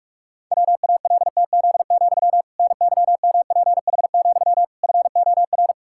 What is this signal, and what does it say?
Morse code, heard as a keyed tone, about 48 wpm, 700 Hz. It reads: WALTZ, NYMPH, FOR
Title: Pangram Challenge v2 - 45wpm